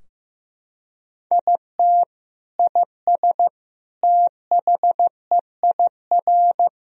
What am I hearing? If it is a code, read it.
Morse code, heard as a keyed tone, about 15 wpm, 700 Hz. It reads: IT IS THEIR
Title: Sentences from Top 1000 Words - Review Part 2 - 15wpm